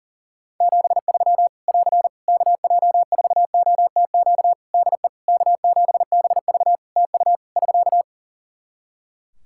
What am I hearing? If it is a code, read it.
Morse code, heard as a keyed tone, about 40 wpm, 700 Hz. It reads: 73 <AR> KJ4OTQ DE K7BV TU <SK>
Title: K1USN SST QSO Practice 40wpm